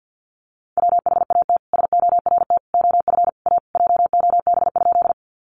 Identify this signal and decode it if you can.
Morse code, heard as a keyed tone, about 50 wpm, 700 Hz. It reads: WHAT SORT OF A JOB?
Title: Sentences from Top 1300 Words 50wpm